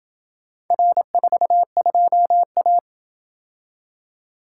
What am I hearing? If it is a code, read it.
Morse code, heard as a keyed tone, about 28 wpm, 700 Hz. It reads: R42A